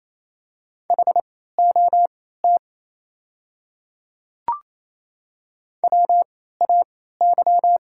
Morse code, heard as a keyed tone, about 29 wpm, spaced as for 16 wpm, 690 Hz. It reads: HOT WAY
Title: 3x Character Spacing - Top 100 Words - 28wpm